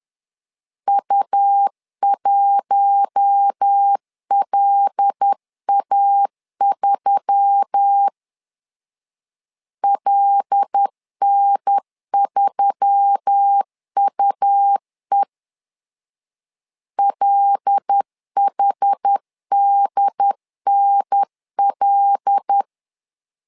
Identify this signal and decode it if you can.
Morse code, heard as a keyed tone, about 10 wpm, 800 Hz. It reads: U1LA3 LN3UE LHDNL
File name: LEZIONI / 5 LBVC / 5 L 10wpm 2259